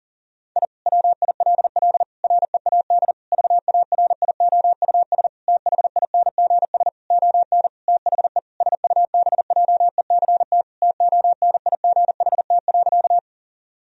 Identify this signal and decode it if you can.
Morse code, heard as a keyed tone, about 40 wpm, 700 Hz. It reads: I WILL READ VARIOUS THINGS ON THE SUBJECT TONIGHT.
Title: Sentences from Top 1000 Words - Review Part 4 - 40wpm